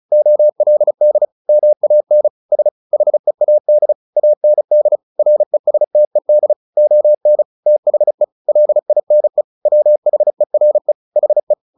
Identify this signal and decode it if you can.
Morse code, heard as a keyed tone, about 35 wpm, 600 Hz. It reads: OLD MAN S HEAD AND RESTED ON THE LINE WHERE HE
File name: the old man and the sea / the old man and the sea 1 35wpm